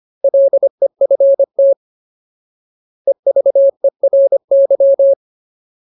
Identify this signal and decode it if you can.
Morse code, heard as a keyed tone, about 25 wpm, 550 Hz. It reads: LEFT EVERY
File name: Ultimate Word List / 25WPM Ultimate